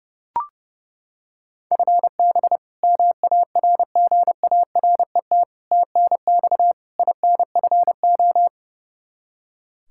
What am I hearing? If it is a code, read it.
Morse code, heard as a keyed tone, about 31 wpm, 700 Hz. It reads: FB MARGARET TNX INFO
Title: Practice QSO - Exchange 3 - More Info - 30wpm